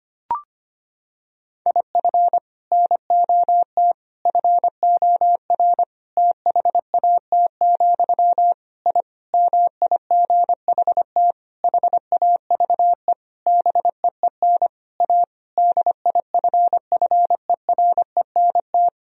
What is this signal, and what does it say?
Morse code, heard as a keyed tone, about 25 wpm, 700 Hz. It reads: IF NOT FOR THAT, I MIGHT HAVE BEEN A DIFFERENT
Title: Sentences from Top 1000 Words - Review Part 3 - 25wpm